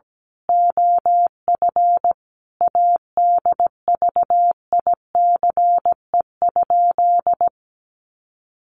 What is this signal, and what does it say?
Morse code, heard as a keyed tone, about 17 wpm, 700 Hz. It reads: OF ADVICE?
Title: Sentences from Top 1100 Words 17wpm